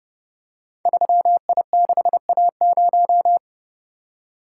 Morse code, heard as a keyed tone, about 30 wpm, 700 Hz. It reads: 3I6A0
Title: Code Groups - 5 Letter:Numbers - 30wpm